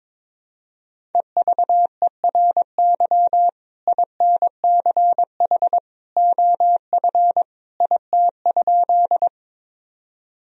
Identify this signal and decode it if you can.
Morse code, heard as a keyed tone, about 22 wpm, 700 Hz. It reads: EVERY INCH OF IT?
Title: Sentences from Top 1000 Words - Review Part 3 - 22wpm